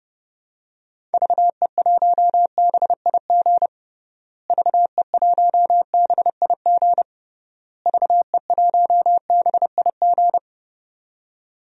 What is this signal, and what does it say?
Morse code, heard as a keyed tone, about 30 wpm, 700 Hz. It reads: VE1BIG VE1BIG VE1BIG
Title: DXCC-100 Call Signs - 6 Characters - Repeated 3x - 30wpm